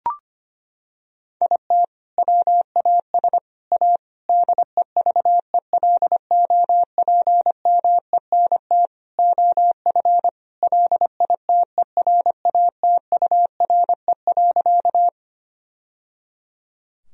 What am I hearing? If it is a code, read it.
Morse code, heard as a keyed tone, about 25 wpm, 700 Hz. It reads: IT WAS A DEVELOPMENT OF LITERATURE.
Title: Sentences from Top 1000 Words - Review Part 4 - 25wpm